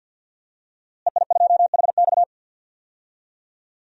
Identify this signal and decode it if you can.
Morse code, heard as a keyed tone, about 47 wpm, 700 Hz. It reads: EI2HX
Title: DXCC-100 Call Signs - 5 Characters - Repeated 2x - 50wpm